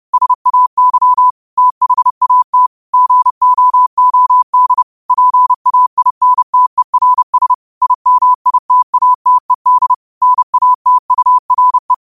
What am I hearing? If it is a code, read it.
Morse code, heard as a keyed tone, about 30 wpm, 1000 Hz. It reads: SAY THAT GOOD PAINTERS IMITATED NATURE
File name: qotdfiles / 30wpm